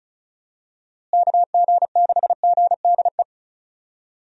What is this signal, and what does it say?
Morse code, heard as a keyed tone, about 36 wpm, 700 Hz. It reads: KG6GDE
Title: USA Call Signs - 6 Characters - Repeated 2x - 35wpm